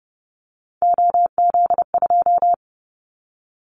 Morse code, heard as a keyed tone, about 30 wpm, 700 Hz. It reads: OZ2
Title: Callsign Prefixes for Top 100 DXCC - 30wpm